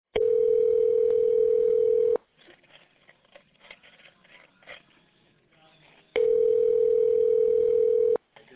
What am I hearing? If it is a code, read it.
Ringback tone (North America), 440 + 480 Hz.